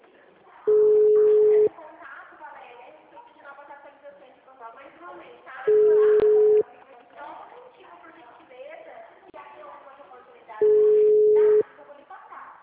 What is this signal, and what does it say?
Ringback tone (Europe, 425 Hz), with speech.